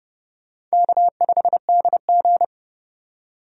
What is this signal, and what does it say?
Morse code, heard as a keyed tone, about 32 wpm, 700 Hz. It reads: K5DG
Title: USA Call Signs - 4 Characters - 30wpm